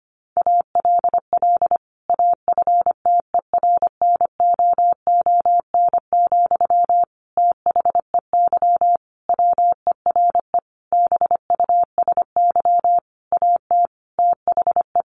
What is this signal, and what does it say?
Morse code, heard as a keyed tone, about 25 wpm, 700 Hz. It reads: ALL AFTERNOON, THEY WERE BUSY AT THE
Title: Sentences from Top 1500 Words 25wpm